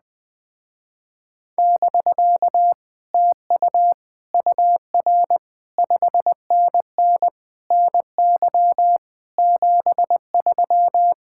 Morse code, heard as a keyed tone, about 20 wpm, 700 Hz. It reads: <BK> TU UR 5NN NY 73